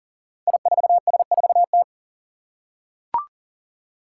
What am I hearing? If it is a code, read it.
Morse code, heard as a keyed tone, about 40 wpm, 700 Hz. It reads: I4S4T
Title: Five Challenging Letter-Number Combinations 40wpm